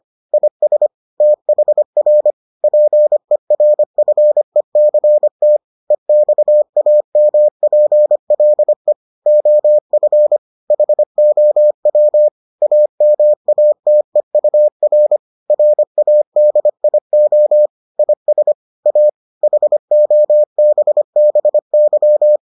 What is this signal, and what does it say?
Morse code, heard as a keyed tone, about 25 wpm, 600 Hz. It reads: IS THR PERFECT EXAMPLE OF HOW AMATEUR RADIO IS A HOBBY